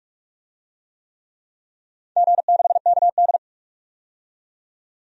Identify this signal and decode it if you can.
Morse code, heard as a keyed tone, about 46 wpm, 700 Hz. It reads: G6KD
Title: DXCC-100 Call Signs - 4 Characters - 45wpm